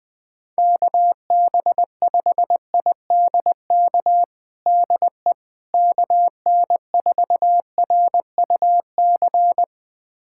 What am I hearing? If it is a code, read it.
Morse code, heard as a keyed tone, about 20 wpm, 700 Hz. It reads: KB5IDK DE KN4RUC
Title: USA Callsign DE Callsign - 6 Characters - 20wpm